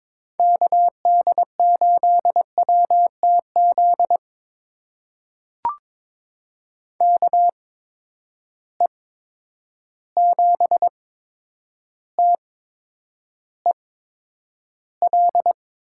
Morse code, heard as a keyed tone, about 22 wpm, spaced as for 9 wpm, 700 Hz. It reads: KD8WTZ KE7TEL
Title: USA Call Signs - 8x Character Spacing - 22wpm